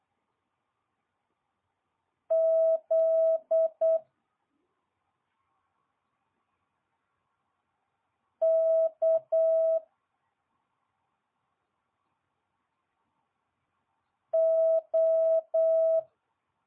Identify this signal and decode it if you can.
Morse code, heard as a keyed tone, about 8 wpm, 650 Hz. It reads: Z K O
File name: jh8cbh / 2022cw